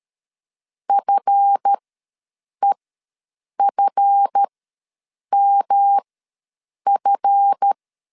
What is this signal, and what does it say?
Morse code, heard as a keyed tone, about 13 wpm, spaced as for 7 wpm, 800 Hz. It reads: FEFMF
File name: LEZIONI / 7 GF9Z / 7 GF ws 12wpm 3140